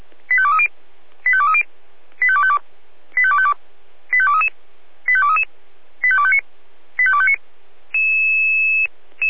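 Five-tone selective call (ZVEI), heard as quick runs of five tones, tones of about 70 ms. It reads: 85230 85230 85252 85252 85233 85233 85259 85259
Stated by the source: SDR-received ZVEI five-tone selective call, address 85232.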